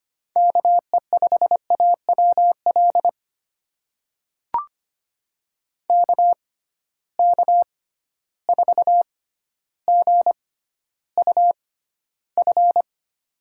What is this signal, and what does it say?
Morse code, heard as a keyed tone, about 25 wpm, spaced as for 13 wpm, 700 Hz. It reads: KE5AWL KK4GUF
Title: USA Call Signs - 6 Characters - 6x Character Spacing - 25wpm